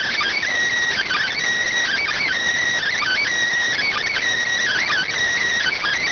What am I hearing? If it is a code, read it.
Emergency Alert System (SAME) tones, heard as harsh buzzing bursts. It is heard from 0.9 to 5.1 s.